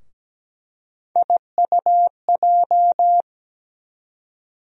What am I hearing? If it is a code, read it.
Morse code, heard as a keyed tone, about 17 wpm, 700 Hz. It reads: IUJ
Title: USA Call Sign Postfixes - 3 Letters 17wpm